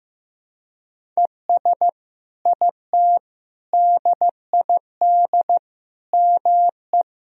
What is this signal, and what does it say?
Morse code, heard as a keyed tone, about 15 wpm, 700 Hz. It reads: ES IT DID ME
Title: Sentences from Top 200 Words and Abbreviations 15wpm